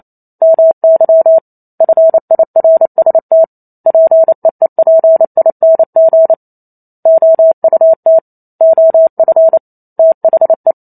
Morse code, heard as a keyed tone, about 28 wpm, 650 Hz. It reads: MY FIRST PEEPING OUT OF THE
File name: Chapter 2 - 28 WPM